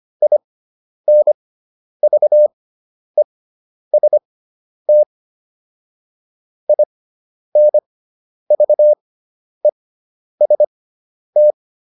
Morse code, heard as a keyed tone, about 25 wpm, spaced as for 10 wpm, 600 Hz. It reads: INVEST INVEST